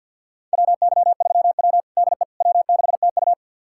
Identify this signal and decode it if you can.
Morse code, heard as a keyed tone, about 49 wpm, 700 Hz. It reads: WY3W DE W6TU
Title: USA Callsign DE Callsign - 4 Characters - 50wpm